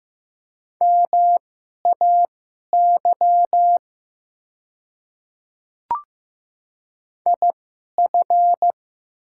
Morse code, heard as a keyed tone, about 15 wpm, spaced as for 11 wpm, 700 Hz. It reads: MAY IF
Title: 2x Character Spacing - Top 200 Words - 15wpm